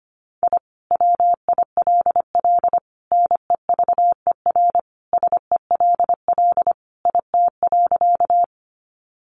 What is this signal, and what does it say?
Morse code, heard as a keyed tone, about 25 wpm, 700 Hz. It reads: I WILL NEVER SELL IT.
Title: Sentences from Top 1100 Words 25wpm